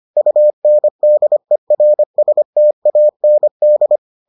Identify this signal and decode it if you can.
Morse code, heard as a keyed tone, about 25 wpm, 600 Hz. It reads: UNDERSTAND